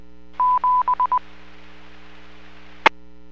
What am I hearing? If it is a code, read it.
Morse code, heard as a keyed tone, about 20 wpm, 1000 Hz. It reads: 7 E